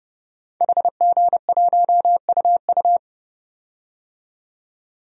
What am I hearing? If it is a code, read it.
Morse code, heard as a keyed tone, about 29 wpm, 700 Hz. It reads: HG1UU
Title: DXCC-100 Call Signs - 5 Characters - 30wpm